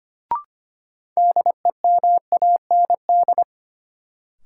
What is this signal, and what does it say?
Morse code, heard as a keyed tone, about 25 wpm, 700 Hz. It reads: DEMAND
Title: Sentences from Top 800 Words - New Word Drill - 25wpm